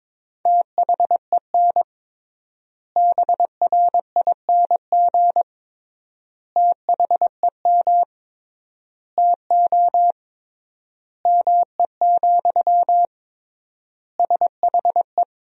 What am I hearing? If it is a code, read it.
Morse code, heard as a keyed tone, about 21 wpm, 700 Hz. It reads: THEN BRING THEM TO ME, SHE
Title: 3x Word-Spacing - Sentences from Top 500 Words - 22wpm